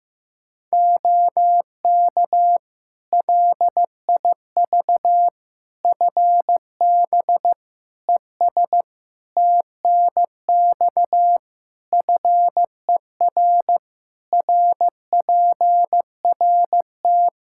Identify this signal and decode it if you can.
Morse code, heard as a keyed tone, about 14 wpm, 700 Hz. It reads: OK LIV FB ES TNX FER RPRT
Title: Practice QSO - Exchange 2 - Info - 15wpm